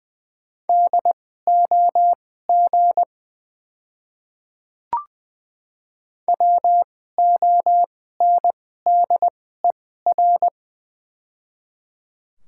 Morse code, heard as a keyed tone, about 20 wpm, spaced as for 14 wpm, 700 Hz. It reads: DOG WONDER
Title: 2x Character Spacing - Top 500 Words - 20wpm